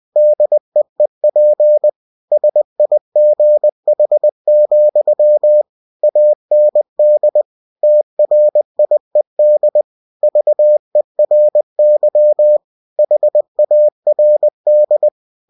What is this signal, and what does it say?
Morse code, heard as a keyed tone, about 20 wpm, 600 Hz. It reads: DEEP SIGH, AND TRIED VERY HARD
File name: winnie-20wpm